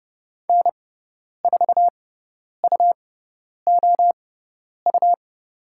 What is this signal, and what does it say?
Morse code, heard as a keyed tone, about 30 wpm, spaced as for 10 wpm, 700 Hz. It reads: N4UOU
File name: Lesson 18g - Letter m - Call Signs 30wpm